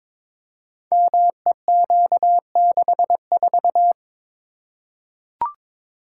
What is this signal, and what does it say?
Morse code, heard as a keyed tone, about 22 wpm, 700 Hz. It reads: MEQ64
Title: Code Groups - 5 Letter:Numbers - 22wpm